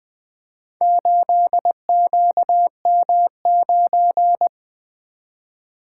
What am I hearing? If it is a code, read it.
Morse code, heard as a keyed tone, about 20 wpm, 700 Hz. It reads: 8QM9
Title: Four Letter-Number Combinations - 20wpm